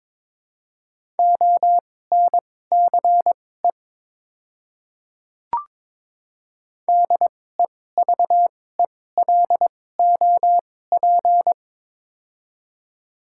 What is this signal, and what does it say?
Morse code, heard as a keyed tone, about 22 wpm, spaced as for 16 wpm, 700 Hz. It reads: ONCE DEVELOP
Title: 2x Character Spacing - Top 500 Words - 22wpm